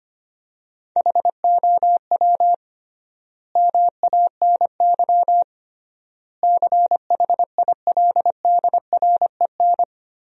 Morse code, heard as a keyed tone, about 25 wpm, 700 Hz. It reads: HOW MANY CHILDREN
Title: 3x Word-Spacing - Sentences from Top 400 Words - 25wpm